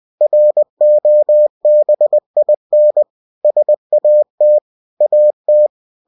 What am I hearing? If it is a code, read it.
Morse code, heard as a keyed tone, about 20 wpm, 600 Hz. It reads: ROBIN SAT AT
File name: cw-books / winnie-the-pooh / winnie-20wpm-page-0106